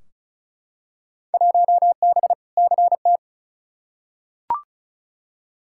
Morse code, heard as a keyed tone, about 36 wpm, 700 Hz. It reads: 1B CT
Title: Field Day Exchanges Encore - 35wpm